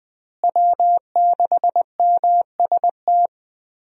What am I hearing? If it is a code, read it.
Morse code, heard as a keyed tone, about 20 wpm, 700 Hz. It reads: W6MST
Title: USA Call Signs - 5 Characters - Repeated 3x - 20wpm